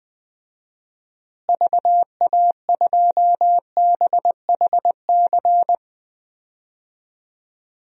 Morse code, heard as a keyed tone, about 20 wpm, 700 Hz. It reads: VA2BHC
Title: DXCC-100 Call Signs - 6 Characters - Repeated 2x - 20wpm